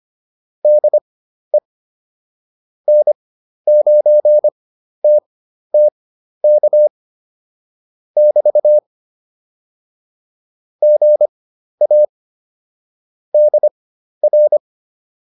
Morse code, heard as a keyed tone, about 25 wpm, spaced as for 12 wpm, 600 Hz. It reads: DE N9TTK <BT> GA DR